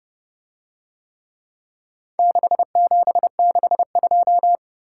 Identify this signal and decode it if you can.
Morse code, heard as a keyed tone, about 27 wpm, 700 Hz. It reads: 6762